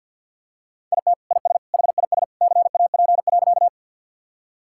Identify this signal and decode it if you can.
Morse code, heard as a keyed tone, about 50 wpm, 700 Hz. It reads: IT IS HIS CAP.
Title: Sentences from Top 1600 Words 50wpm